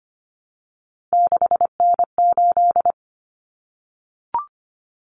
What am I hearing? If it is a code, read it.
Morse code, heard as a keyed tone, about 25 wpm, 700 Hz. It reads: 6N8
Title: Common Cut Numbers - 3-Digit 25wpm